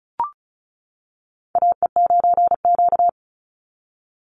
Morse code, heard as a keyed tone, about 34 wpm, 700 Hz. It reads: AE9Q
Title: NA SOTA Chaser Callsigns 35wpm